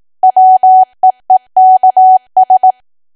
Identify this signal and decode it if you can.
Morse code, heard as a keyed tone, about 18 wpm, 750 Hz. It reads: WEEKS
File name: cwa / W1AW-Level2 / 18WPM